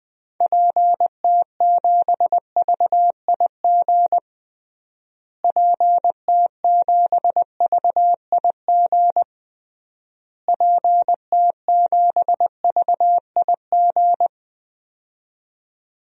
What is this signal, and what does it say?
Morse code, heard as a keyed tone, about 20 wpm, 700 Hz. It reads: PT7VIG PT7VIG PT7VIG
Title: DXCC-100 Call Signs - 6 Characters - Repeated 3x - 20wpm